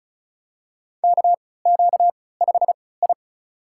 Morse code, heard as a keyed tone, about 36 wpm, spaced as for 20 wpm, 700 Hz. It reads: KQ5I
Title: USA Call Signs - 3x Character Spacing - 35wpm